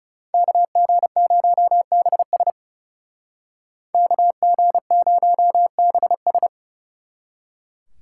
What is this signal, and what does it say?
Morse code, heard as a keyed tone, about 39 wpm, 700 Hz. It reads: KG0BS KG0BS